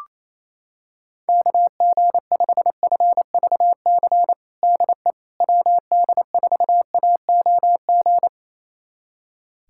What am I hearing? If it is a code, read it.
Morse code, heard as a keyed tone, about 28 wpm, 700 Hz. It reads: KG5FVC DE WD4AOG
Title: USA Callsign DE Callsign 28wpm